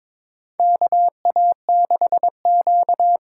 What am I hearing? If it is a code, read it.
Morse code, heard as a keyed tone, about 22 wpm, 700 Hz. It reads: KA6Q
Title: USA Call Signs - 6 Characters - 6x Character Spacing - 22wpm